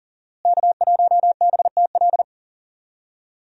Morse code, heard as a keyed tone, about 40 wpm, 700 Hz. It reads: K1BTL
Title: USA Call Signs - 5 Characters - Repeated 2x - 40wpm